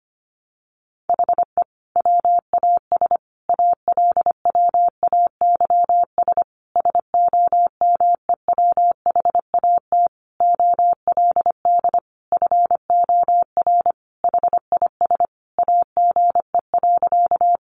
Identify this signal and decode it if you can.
Morse code, heard as a keyed tone, about 25 wpm, 700 Hz. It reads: HE WAS ALWAYS SOMEWHAT OLD FOR HIS AGE.
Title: Sentences from Top 1300 Words 25wpm